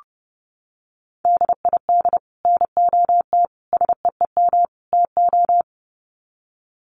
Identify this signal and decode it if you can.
Morse code, heard as a keyed tone, about 30 wpm, 700 Hz. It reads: DID NOT SEEM TO
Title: Sets of 4 Words 30wpm